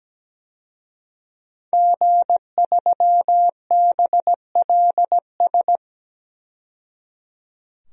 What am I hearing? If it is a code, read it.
Morse code, heard as a keyed tone, about 17 wpm, 700 Hz. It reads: G3BLS
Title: DXCC-100 Call Signs - 5 Characters - 17wpm